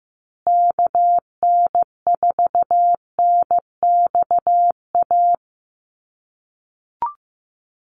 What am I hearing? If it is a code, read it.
Morse code, heard as a keyed tone, about 15 wpm, 700 Hz. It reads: KN4NXA
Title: USA Call Signs Repeated 3x 15wpm